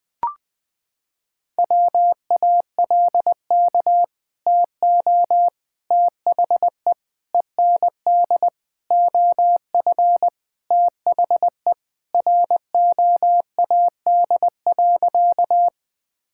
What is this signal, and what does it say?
Morse code, heard as a keyed tone, about 21 wpm, 700 Hz. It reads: WALK TO THE END OF THE ROAD.
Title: Sentences with 4-Word Phrases - 20wpm